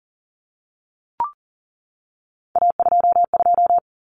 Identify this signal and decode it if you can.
Morse code, heard as a keyed tone, about 40 wpm, 700 Hz. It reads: A22
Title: Common Cut Numbers - 3-Digit 40wpm